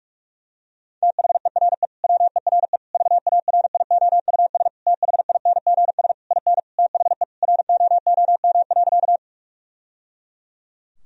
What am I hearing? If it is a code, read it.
Morse code, heard as a keyed tone, about 46 wpm, 700 Hz. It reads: THERE WERE VARIOUS THINGS IN THE ROOM.
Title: Sentences from Top 1000 Words - Review Part 4 - 45wpm